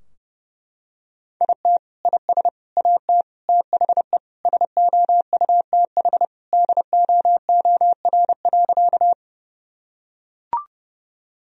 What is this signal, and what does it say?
Morse code, heard as a keyed tone, about 30 wpm, 700 Hz. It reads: IT IS AT THE SOUTH DOOR.
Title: Sentences from Top 1000 Words - Review Part 2 - 30wpm